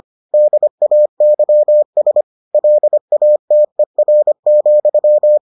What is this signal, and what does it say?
Morse code, heard as a keyed tone, about 25 wpm, 600 Hz. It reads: DAYS LATER,